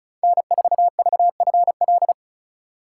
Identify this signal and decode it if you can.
Morse code, heard as a keyed tone, about 34 wpm, 700 Hz. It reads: N4VFL
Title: USA Call Signs - 2x Character Spacing - 35wpm